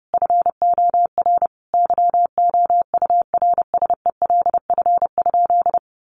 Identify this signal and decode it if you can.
Morse code, heard as a keyed tone, about 30 wpm, 700 Hz. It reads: FOR YOURSELF?
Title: Sentences from Top 1500 Words 30wpm